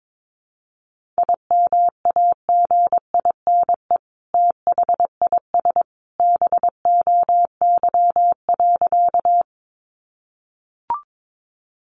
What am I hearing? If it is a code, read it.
Morse code, heard as a keyed tone, about 22 wpm, 700 Hz. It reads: IMAGINE THIS BOY.
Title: Sentences from Top 1400 Words 22wpm